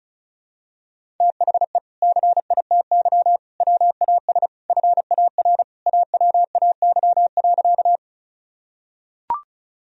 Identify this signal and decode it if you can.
Morse code, heard as a keyed tone, about 34 wpm, 700 Hz. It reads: THE CITY WAS FAR AWAY.
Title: Sentences from Top 1000 Words - Review Part 1 - 35wpm